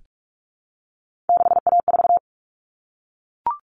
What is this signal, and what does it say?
Morse code, heard as a keyed tone, about 44 wpm, 700 Hz. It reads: 6A4
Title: Common Cut Numbers - 3-Digit 45wpm